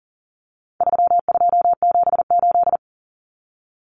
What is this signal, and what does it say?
Morse code, heard as a keyed tone, about 40 wpm, 700 Hz. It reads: 3278
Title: Common Cut Numbers - 4-Digit 40wpm